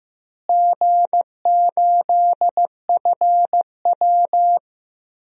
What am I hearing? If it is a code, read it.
Morse code, heard as a keyed tone, about 16 wpm, 700 Hz. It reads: G8FW
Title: DXCC-100 Call Signs - 4 Characters - Repeated 3x - 15wpm